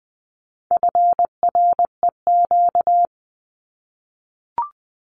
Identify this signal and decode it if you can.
Morse code, heard as a keyed tone, about 20 wpm, 700 Hz. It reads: FREQ
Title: Abbreviations Repeated 3x - 20wpm